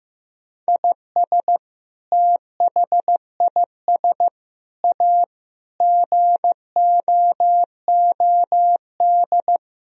Morse code, heard as a keyed tone, about 15 wpm, 700 Hz. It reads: IS THIS A GOOD
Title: Sentences from Top 1000 Words - Review Part 3 - 15wpm